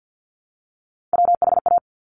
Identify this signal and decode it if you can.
Morse code, heard as a keyed tone, about 50 wpm, 700 Hz. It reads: WHA
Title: Top Three Letter Combinations in Words 50wpm